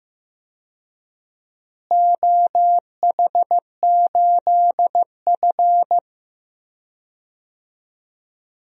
Morse code, heard as a keyed tone, about 15 wpm, 700 Hz. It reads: OH8F
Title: DXCC-100 Call Signs - 4 Characters - Repeated 3x - 15wpm